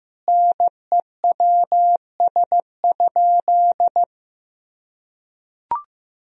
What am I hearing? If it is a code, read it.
Morse code, heard as a keyed tone, about 15 wpm, 700 Hz. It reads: NEWS?
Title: Sentences from Top 1000 Words - Review Part 4 - 15wpm